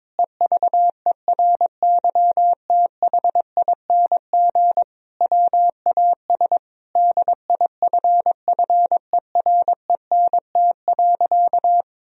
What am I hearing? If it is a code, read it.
Morse code, heard as a keyed tone, about 22 wpm, 700 Hz. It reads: EVERYTHING WAS DIFFERENT.
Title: Sentences from Top 1000 Words - Review Part 3 - 22wpm